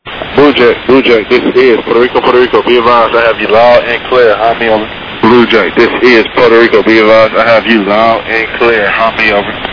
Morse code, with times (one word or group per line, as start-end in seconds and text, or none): none